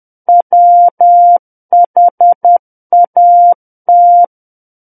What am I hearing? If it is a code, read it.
Morse code, heard as a keyed tone, about 10 wpm, 700 Hz. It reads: WHAT
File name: Book One / 01-Chapter 1 - 10 WPM